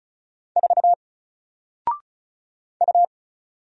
Morse code, heard as a keyed tone, about 34 wpm, 700 Hz. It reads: 4 U
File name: Single-Letter-Number-35wpm